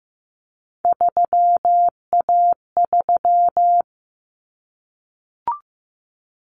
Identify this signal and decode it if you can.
Morse code, heard as a keyed tone, about 15 wpm, 700 Hz. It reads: 3A3
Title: Common Cut Numbers - 3-Digit 15wpm